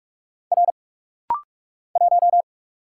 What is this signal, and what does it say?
Morse code, heard as a keyed tone, about 46 wpm, 700 Hz. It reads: R 1
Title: Single Letter-Number Alt - Rapid-Fire 45wpm